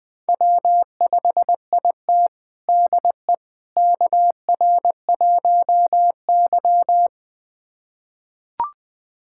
Morse code, heard as a keyed tone, about 21 wpm, 700 Hz. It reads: W5IT DE KR1Y
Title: USA Callsign DE Callsign - 4 Characters - 20wpm